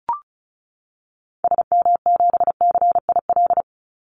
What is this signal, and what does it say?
Morse code, heard as a keyed tone, about 36 wpm, 700 Hz. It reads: SM7CIL
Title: Call Signs - 35wpm